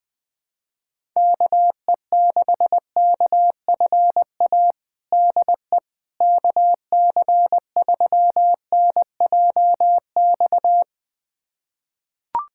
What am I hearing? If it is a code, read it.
Morse code, heard as a keyed tone, about 20 wpm, 700 Hz. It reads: KE6KFA DE KC3NJX
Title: USA Callsign DE Callsign - 6 Characters - 20wpm